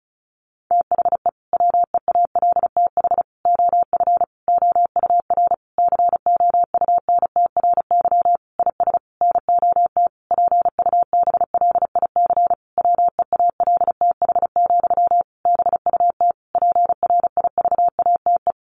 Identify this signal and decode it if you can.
Morse code, heard as a keyed tone, about 34 wpm, 700 Hz. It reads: THE WEALTH OF OUR COUNTRY IS NOT PUBLIC WEALTH, BUT PRIVATE
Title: Sentences from Top 1200 Words 35wpm